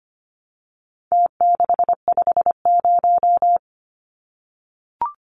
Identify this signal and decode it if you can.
Morse code, heard as a keyed tone, about 25 wpm, 700 Hz. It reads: T650
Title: Numbers - Contest Style - 4-Digit 25wpm